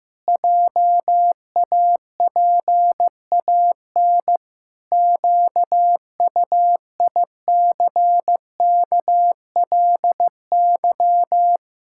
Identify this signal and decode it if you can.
Morse code, heard as a keyed tone, about 15 wpm, 700 Hz. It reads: JAPAN QUICKLY
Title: Pangram Challenge v2 - 15wpm